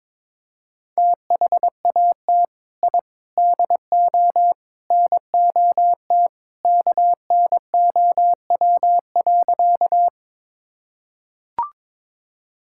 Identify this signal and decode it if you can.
Morse code, heard as a keyed tone, about 22 wpm, 700 Hz. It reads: THAT I DO NOT KNOW.
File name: practice / 22wpm / Sentences with 4-Word Phrases - 22wpm